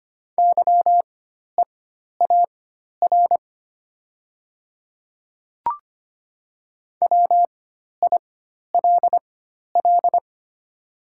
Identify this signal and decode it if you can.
Morse code, heard as a keyed tone, about 25 wpm, spaced as for 12 wpm, 700 Hz. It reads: YEAR WILL
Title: 4x Character Spacing - Top 500 Words - 25wpm